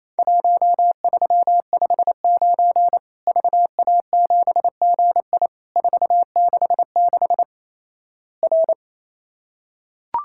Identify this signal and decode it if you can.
Morse code, heard as a keyed tone, about 28 wpm, 700 Hz. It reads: 1359 VA7GI 466 R
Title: Worked All Europe DX Contest - QTC Exchange - 28wpm